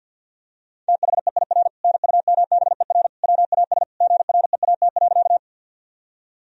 Morse code, heard as a keyed tone, about 50 wpm, 700 Hz. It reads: THEIR NUMBER WAS GREAT.
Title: Sentences from Top 1000 Words - Review Part 1 - 50wpm